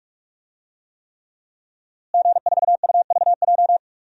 Morse code, heard as a keyed tone, about 46 wpm, 700 Hz. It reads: G4UVJ
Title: DXCC-100 Call Signs - 5 Characters - Repeated 3x - 45wpm